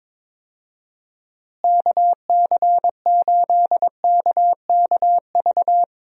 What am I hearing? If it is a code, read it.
Morse code, heard as a keyed tone, about 22 wpm, 700 Hz. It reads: KC8KKV
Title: USA Call Signs - 6 Characters - Repeated 2x - 22wpm